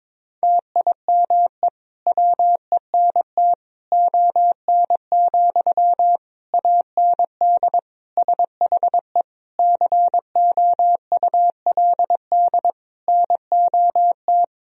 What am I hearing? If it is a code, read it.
Morse code, heard as a keyed tone, about 22 wpm, 690 Hz. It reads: TIME WENT ON, AND SHE COULD NOT
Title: Sentences from Top 1000 Words - Review Part 3 - 22wpm